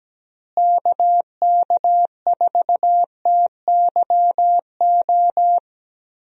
Morse code, heard as a keyed tone, about 17 wpm, 700 Hz. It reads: KK4TYO
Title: USA Call Signs - 6 Characters - Repeated 3x - 17wpm